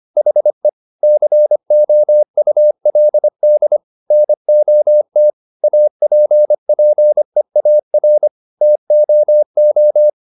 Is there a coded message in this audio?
Yes: Morse code from 0.2 to 10.1 s, heard as a keyed tone, about 25 wpm, 600 Hz. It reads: HE COULD NOT APPEAR TOO